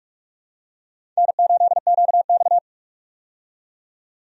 Morse code, heard as a keyed tone, about 46 wpm, 700 Hz. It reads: N8QX
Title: USA Call Signs - 4 Characters - 45wpm